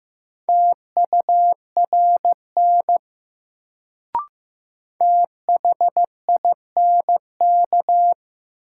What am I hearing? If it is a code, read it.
Morse code, heard as a keyed tone, about 15 wpm, 700 Hz. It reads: TURN THINK
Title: Sentences from Top 200 Words - New Word Drill - 15wpm